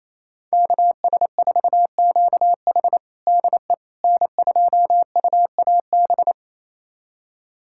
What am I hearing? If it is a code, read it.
Morse code, heard as a keyed tone, about 28 wpm, 700 Hz. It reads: KS4QH DE N2UAB
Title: USA Callsign DE Callsign - 5 Characters - 28wpm